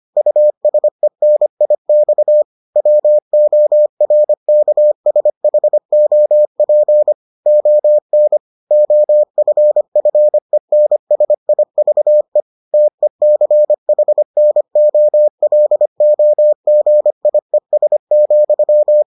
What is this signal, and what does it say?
Morse code, heard as a keyed tone, about 25 wpm, 600 Hz. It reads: USENIX WORKSHOP ON OFFENSIVE TECHNOLOGIES,